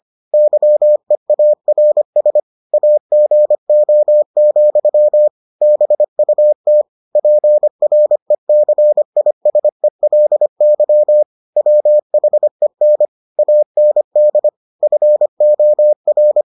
Morse code, heard as a keyed tone, about 25 wpm, 600 Hz. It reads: YEARS AGO, BUT PRECISELY WHEN AND FOR